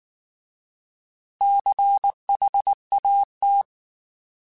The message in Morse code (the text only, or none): CHAT